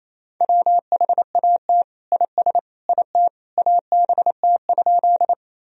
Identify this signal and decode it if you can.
Morse code, heard as a keyed tone, about 28 wpm, 700 Hz. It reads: WHAT IS IT ABT?
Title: Sentences from Top 100 Words and Abbreviations 28wpm